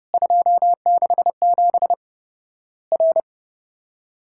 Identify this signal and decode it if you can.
Morse code, heard as a keyed tone, about 30 wpm, 700 Hz. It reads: 267 R
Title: Worked All Europe DX Contest - QTC Exchange - 30wpm